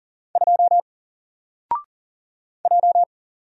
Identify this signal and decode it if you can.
Morse code, heard as a keyed tone, about 40 wpm, 700 Hz. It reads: 2 J
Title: Single Letter-Number Alt 40wpm